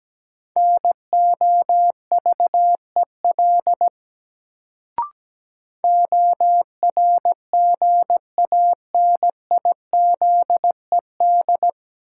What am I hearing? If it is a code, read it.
Morse code, heard as a keyed tone, about 17 wpm, 700 Hz. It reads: NOVEL ORGANIZED
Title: Sentences from Top 1300 Words - New Word Drill Alt - 17wpm